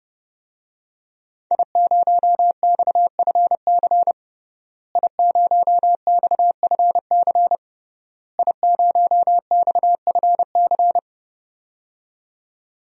Morse code, heard as a keyed tone, about 30 wpm, 700 Hz. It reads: I0XFC I0XFC I0XFC
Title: DXCC-100 Call Signs - 5 Characters - Repeated 3x - 30wpm